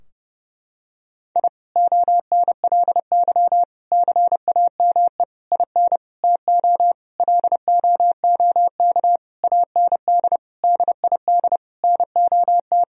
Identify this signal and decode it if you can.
Morse code, heard as a keyed tone, about 32 wpm, 700 Hz. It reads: I ONLY CAME IN TO LOOK AND DID NOT
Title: Sentences from Top 1000 Words - Review Part 2 - 30wpm